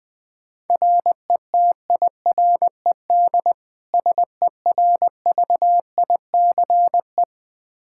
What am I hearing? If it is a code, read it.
Morse code, heard as a keyed tone, about 20 wpm, 700 Hz. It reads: RETIRED SERVICE
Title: K1USN SST QSO Practice 20wpm